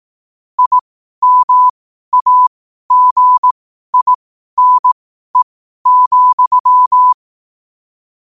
Morse code, heard as a keyed tone, about 18 wpm, spaced as for 12 wpm, 1000 Hz. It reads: IMAGINE,
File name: qotdfiles / 12wpm